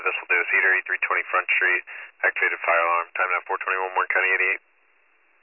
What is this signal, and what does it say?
Emergency Alert System (SAME) tones (header bursts), heard as harsh buzzing bursts, with speech. It is heard from 0.2 to 4.6 s.